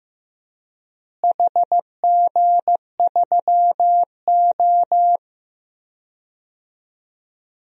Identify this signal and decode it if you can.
Morse code, heard as a keyed tone, about 15 wpm, 700 Hz. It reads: HG3O
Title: DXCC-100 Call Signs - 4 Characters - Repeated 3x - 15wpm